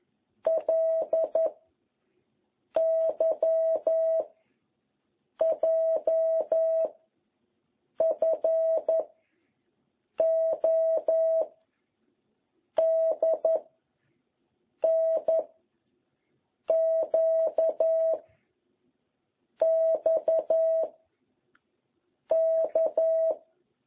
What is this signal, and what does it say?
Morse code, heard as a keyed tone, about 11 wpm, 660 Hz. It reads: L Y J F O D N Q X K